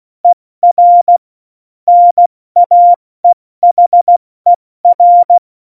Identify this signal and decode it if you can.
Morse code, heard as a keyed tone, about 16 wpm, 700 Hz. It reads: ER NAEHER
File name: Morsekurs / Witze / Witz 13. Oktober 2020, 70 F80